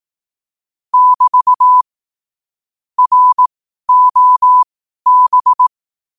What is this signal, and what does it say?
Morse code, heard as a keyed tone, about 18 wpm, spaced as for 12 wpm, 1000 Hz. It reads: <BT> ROB
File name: 12wpm